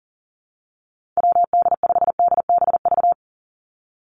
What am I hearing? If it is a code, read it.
Morse code, heard as a keyed tone, about 40 wpm, 700 Hz. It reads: WD5DBV
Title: Call Signs - 40wpm